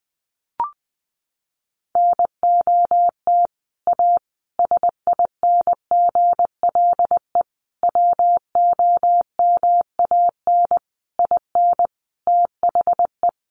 Morse code, heard as a keyed tone, about 20 wpm, 700 Hz. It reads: NOT A SINGLE WOMAN IN THE
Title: Sentences from Top 1100 Words 20wpm